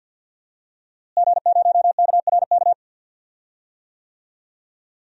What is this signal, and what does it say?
Morse code, heard as a keyed tone, about 52 wpm, 700 Hz. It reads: G0KRK
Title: DXCC-100 Call Signs - 5 Characters - Repeated 3x - 50wpm